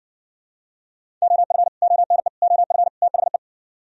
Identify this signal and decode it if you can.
Morse code, heard as a keyed tone, about 59 wpm, 700 Hz. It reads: OF ONE OF THE